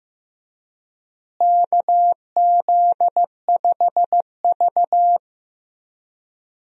Morse code, heard as a keyed tone, about 15 wpm, 700 Hz. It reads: KZ5V
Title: USA Call Signs - 4 Characters - Repeated 3x - 15wpm